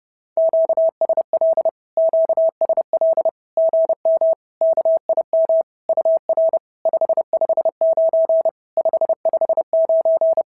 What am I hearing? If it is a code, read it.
Morse code, heard as a keyed tone, about 30 wpm, 650 Hz. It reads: QSL QSL GM KIM UR 559 559
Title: SOTA QSO Practice 30wpm